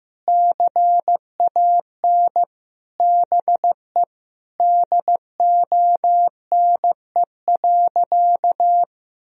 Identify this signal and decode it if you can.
Morse code, heard as a keyed tone, about 15 wpm, 700 Hz. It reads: CAN BE DONE.
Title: Sentences from Top 1000 Words - Review Part 1 - 15wpm